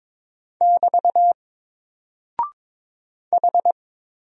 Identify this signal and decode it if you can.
Morse code, heard as a keyed tone, about 22 wpm, 700 Hz. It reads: <BT> H
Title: Single Character (Letter:Number:Punctuation:Prosign) 22wpm